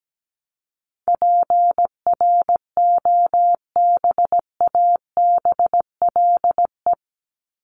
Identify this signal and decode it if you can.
Morse code, heard as a keyed tone, about 17 wpm, 700 Hz. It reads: PROBABLE
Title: LY Suffix - Top 100 Words with Base Words 17wpm